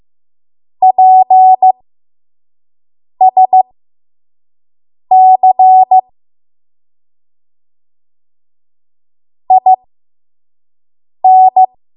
Morse code, heard as a keyed tone, about 15 wpm, spaced as for 5 wpm, 750 Hz. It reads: PSC IN
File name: Morse / 05WPM